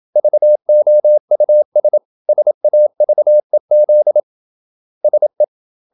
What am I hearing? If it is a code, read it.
Morse code, heard as a keyed tone, about 27 wpm, 600 Hz. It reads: VOUS SAVEZ SE
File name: Ferie QRQ 27wpm